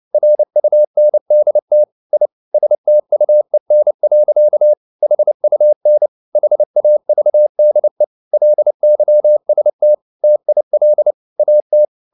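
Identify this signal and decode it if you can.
Morse code, heard as a keyed tone, about 29 wpm, 600 Hz. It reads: RUNDT I STUEN. HUN HAVDE LYST TIL AT
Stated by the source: Fast (QRQ) Morse at 29 wpm, from CW med Gnister.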